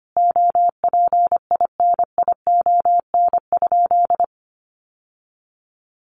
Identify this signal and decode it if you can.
Morse code, heard as a keyed tone, about 25 wpm, 700 Hz. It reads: OPINION?
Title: Sentences from Top 1200 Words 25wpm